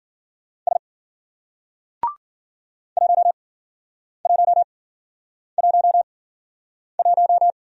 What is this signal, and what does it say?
Morse code, heard as a keyed tone, about 50 wpm, spaced as for 9 wpm, 700 Hz. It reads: S 1111